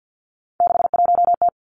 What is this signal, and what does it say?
Morse code, heard as a keyed tone, about 50 wpm, 700 Hz. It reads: 61T
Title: Three Challenging Letter-Number Combinations 50wpm